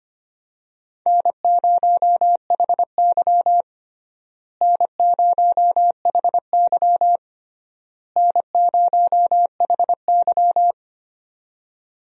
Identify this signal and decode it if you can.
Morse code, heard as a keyed tone, about 25 wpm, 700 Hz. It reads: N0HY N0HY N0HY
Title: USA Call Signs - 4 Characters - Repeated 3x - 25wpm